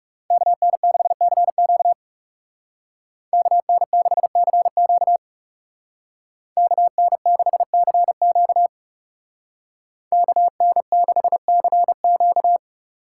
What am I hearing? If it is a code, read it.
Morse code, heard as a keyed tone, about 40 wpm, 700 Hz. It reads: KN6CQ KN6CQ KN6CQ KN6CQ